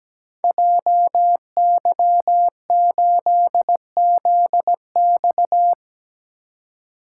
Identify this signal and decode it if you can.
Morse code, heard as a keyed tone, about 17 wpm, 700 Hz. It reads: JY8ZX
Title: Code Groups - 5 Letter:Numbers - 17wpm